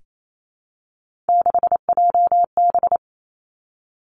Morse code, heard as a keyed tone, about 28 wpm, 700 Hz. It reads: 6JB
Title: Three Challenging Letter-Number Combinations 28wpm